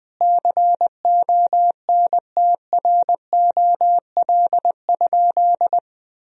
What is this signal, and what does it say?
Morse code, heard as a keyed tone, about 20 wpm, 700 Hz. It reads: CONTROL?
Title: Sentences from Top 1000 Words - Review Part 3 - 20wpm